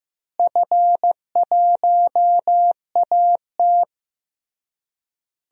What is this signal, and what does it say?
Morse code, heard as a keyed tone, about 16 wpm, 700 Hz. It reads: F1AT
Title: DXCC-100 Call Signs - 4 Characters - Repeated 3x - 15wpm